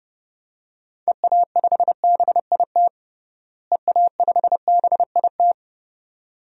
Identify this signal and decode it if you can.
Morse code, heard as a keyed tone, about 30 wpm, 700 Hz. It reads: EA5BIT EA5BIT
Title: DXCC-100 Call Signs - 6 Characters - Repeated 2x - 30wpm